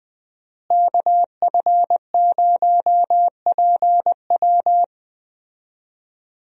Morse code, heard as a keyed tone, about 20 wpm, 700 Hz. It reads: KF0PW
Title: USA Call Signs - 5 Characters - Repeated 3x - 20wpm